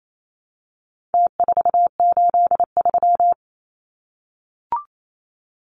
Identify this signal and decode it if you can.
Morse code, heard as a keyed tone, about 28 wpm, 700 Hz. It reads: T483
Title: Numbers - Contest Style - 4-Digit 28wpm